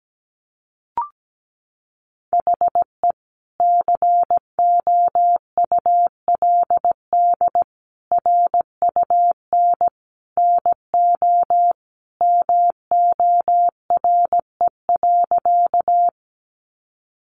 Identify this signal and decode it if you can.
Morse code, heard as a keyed tone, about 17 wpm, 700 Hz. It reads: HE COULD RUN NO MORE.
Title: Sentences from Top 300 Words 17wpm